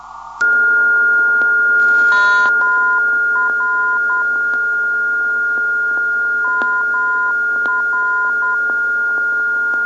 Morse code, heard as a keyed tone, about 10 wpm, 1020 Hz, with noise. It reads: MR MR
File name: ndb zvok / mr 334khz